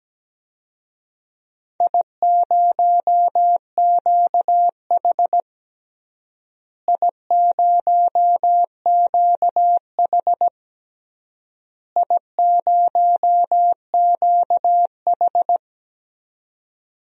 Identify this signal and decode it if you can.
Morse code, heard as a keyed tone, about 17 wpm, 700 Hz. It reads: I0QH I0QH I0QH
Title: DXCC-100 Call Signs - 4 Characters - Repeated 3x - 17wpm